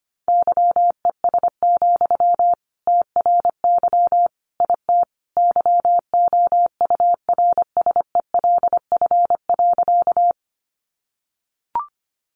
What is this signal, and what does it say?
Morse code, heard as a keyed tone, about 25 wpm, 700 Hz. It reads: YES, TRY IT YOURSELF.
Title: Sentences from Top 600 Words 25wpm